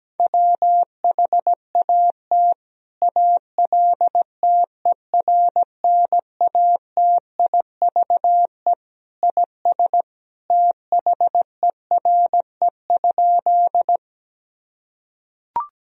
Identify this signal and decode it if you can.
Morse code, heard as a keyed tone, about 17 wpm, 700 Hz. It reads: WHAT ALTERNATIVE IS THERE?
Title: Sentences from Top 1000 Words - Review Part 4 - 17wpm